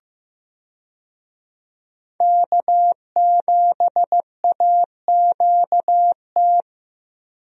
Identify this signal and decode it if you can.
Morse code, heard as a keyed tone, about 15 wpm, 700 Hz. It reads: K7AQT
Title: USA Call Signs - 5 Characters - 15wpm